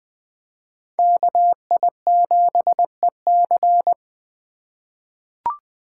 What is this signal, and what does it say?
Morse code, heard as a keyed tone, about 20 wpm, 700 Hz. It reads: KI7EC
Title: USA Call Signs - 2x Character Spacing - 20wpm